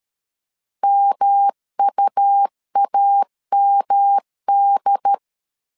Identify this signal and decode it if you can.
Morse code, heard as a keyed tone, about 13 wpm, 800 Hz. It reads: MUAMD
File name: LEZIONI / 6 PKX0 / 6 P 12wpm 2033